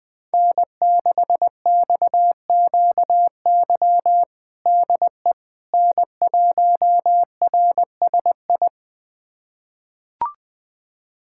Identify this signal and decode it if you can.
Morse code, heard as a keyed tone, about 21 wpm, 700 Hz. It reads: N6XQY DE N1RSI
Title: USA Callsign DE Callsign - 5 Characters - 20wpm